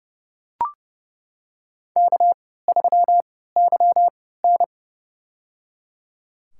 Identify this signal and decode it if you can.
Morse code, heard as a keyed tone, about 30 wpm, spaced as for 17 wpm, 700 Hz. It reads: K3YN
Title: USA Call Signs - 3x Character Spacing - 30wpm